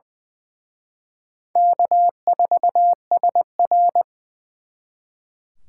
Morse code, heard as a keyed tone, about 24 wpm, 700 Hz. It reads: K4SR
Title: Speed Racing - USA 1x2 Call Signs 20wpm